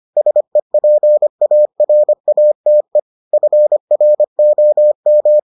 Morse code, heard as a keyed tone, about 25 wpm, 600 Hz. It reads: SEPARATE FROM